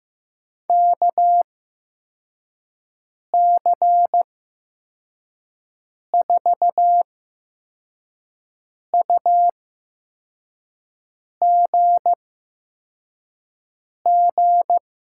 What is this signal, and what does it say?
Morse code, heard as a keyed tone, about 15 wpm, spaced as for 4 wpm, 700 Hz. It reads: KC4UGG
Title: USA Call Signs - 6 Characters - 8x Character Spacing - 15wpm